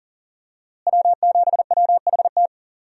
Morse code, heard as a keyed tone, about 40 wpm, 700 Hz. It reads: W7WHT
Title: USA Call Signs - 5 Characters - Repeated 3x - 40wpm